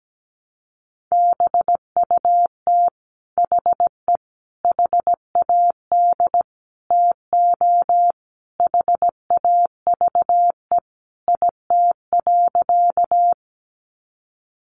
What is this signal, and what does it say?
Morse code, heard as a keyed tone, about 17 wpm, 700 Hz. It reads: BUT HE HAD TO HAVE IT.
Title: Sentences from Top 100 Words 17wpm